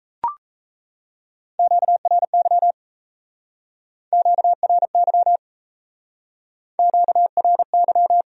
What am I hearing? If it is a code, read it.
Morse code, heard as a keyed tone, about 36 wpm, 700 Hz. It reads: QRY QRY QRY